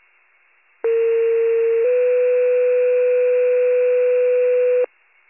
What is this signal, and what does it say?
Two-tone page, heard as a one-second tone, then a three-second tone, about 460 then 500 Hz. It is heard from 0.8 to 4.8 s.